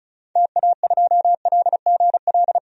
Morse code, heard as a keyed tone, about 34 wpm, 700 Hz. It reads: TA2LGL
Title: DXCC-100 Call Signs - 6 Characters - 35wpm